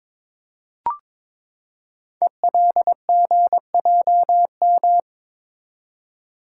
Morse code, heard as a keyed tone, about 22 wpm, 700 Hz. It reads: ELGJM
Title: Code Groups - 5 Letters - 22wpm